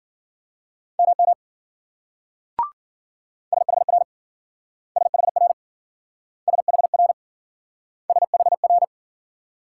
Morse code, heard as a keyed tone, about 51 wpm, 700 Hz. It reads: MM SHR SHR SHR SHR